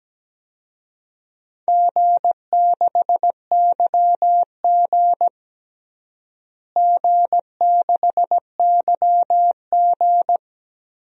Morse code, heard as a keyed tone, about 17 wpm, 700 Hz. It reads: G6YG G6YG
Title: DXCC-100 Call Signs - 4 Characters - Repeated 2x - 17wpm